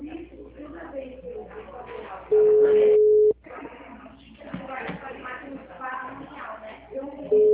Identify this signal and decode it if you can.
Ringback tone (Europe, 425 Hz), with speech.